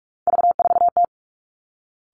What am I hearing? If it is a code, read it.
Morse code, heard as a keyed tone, about 46 wpm, 700 Hz. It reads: V4T
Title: Three Challenging Letter-Number Combinations 45wpm